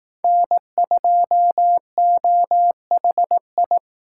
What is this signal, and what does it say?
Morse code, heard as a keyed tone, about 18 wpm, 700 Hz. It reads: N2OHI
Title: Speed Racing - USA Call Signs 15wpm